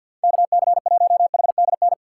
Morse code, heard as a keyed tone, about 47 wpm, 700 Hz. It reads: KC1HDN
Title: USA Call Signs - 6 Characters - 4x Character Spacing - 50wpm